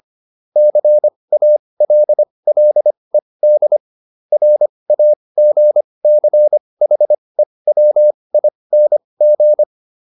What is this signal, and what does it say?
Morse code, heard as a keyed tone, about 25 wpm, spaced as for 20 wpm, 600 Hz. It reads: CALLED RAGCHEWING
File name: SS206 20